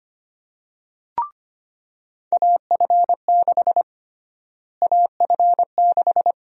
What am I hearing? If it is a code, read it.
Morse code, heard as a keyed tone, about 25 wpm, 700 Hz. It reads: AF6 AF6
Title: USA Call Sign Prefixes - Repeated 2x - 25wpm